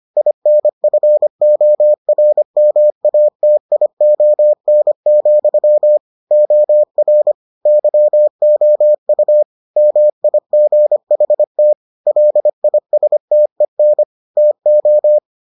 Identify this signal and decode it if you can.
Morse code, heard as a keyed tone, about 25 wpm, 600 Hz. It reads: INFORMATION, OR YOU MIGHT LISTEN TO